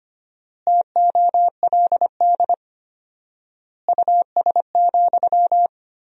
Morse code, heard as a keyed tone, about 25 wpm, 700 Hz. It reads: TOLD US,
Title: 4x Word-Spacing - Sentences from Top 400 Words - 25wpm